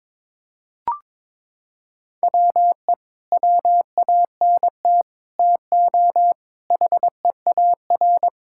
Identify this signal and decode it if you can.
Morse code, heard as a keyed tone, about 22 wpm, 700 Hz. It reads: WE WANT TO HEAR
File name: practice / 22wpm / Sets of 7 Words - 22wpm